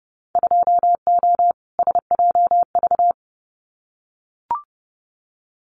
Morse code, heard as a keyed tone, about 30 wpm, 700 Hz. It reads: 2O SJV
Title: Winter Field Day Exchanges Encore 30wpm